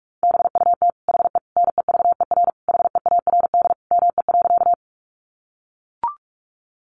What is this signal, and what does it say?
Morse code, heard as a keyed tone, about 44 wpm, 700 Hz. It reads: BUT HE NEVER HEARD ME.
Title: Sentences from Top 400 Words 45wpm